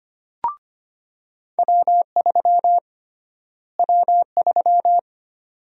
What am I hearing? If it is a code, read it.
Morse code, heard as a keyed tone, about 25 wpm, 700 Hz. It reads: W3 W3
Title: USA Call Sign Prefixes - Repeated 2x - 25wpm